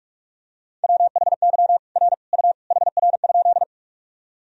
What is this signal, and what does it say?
Morse code, heard as a keyed tone, about 47 wpm, 700 Hz. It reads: WHY R U HR?
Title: Sentences from Top 200 Words and Abbreviations 45wpm